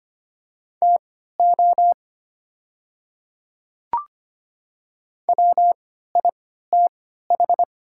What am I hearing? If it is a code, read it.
Morse code, heard as a keyed tone, about 25 wpm, spaced as for 14 wpm, 700 Hz. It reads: TO WITH
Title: 3x Character Spacing - Top 100 Words - 25wpm